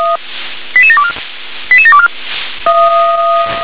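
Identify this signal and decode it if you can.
Five-tone selective call (ZVEI), heard as quick runs of five tones, tones of about 70 ms. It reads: 88724 88724
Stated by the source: ZVEI five-tone call received on SDR.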